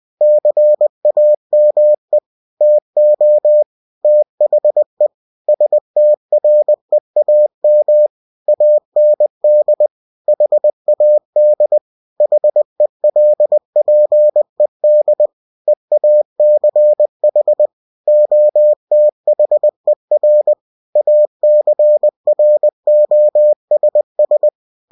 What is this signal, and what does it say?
Morse code, heard as a keyed tone, about 20 wpm, 600 Hz. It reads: CAME TO THE STREAM AND HAD HELPED EACH OTHER ACROSS
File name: cw-books / winnie-the-pooh / winnie-20wpm-page-0034